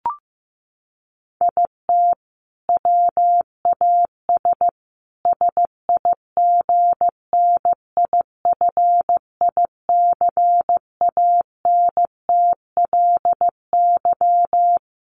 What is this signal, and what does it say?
Morse code, heard as a keyed tone, about 15 wpm, 700 Hz. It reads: IT WAS SIGNIFICANTLY
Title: Sentences from Top 1500 Words 15wpm